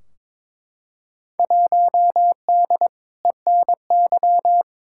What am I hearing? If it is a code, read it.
Morse code, heard as a keyed tone, about 22 wpm, 700 Hz. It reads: 1D ENY
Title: Field Day Exchanges - 22wpm